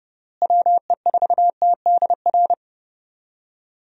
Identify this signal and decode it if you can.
Morse code, heard as a keyed tone, about 32 wpm, 700 Hz. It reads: WE4TDR
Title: USA Call Signs - 6 Characters - Repeated 3x - 30wpm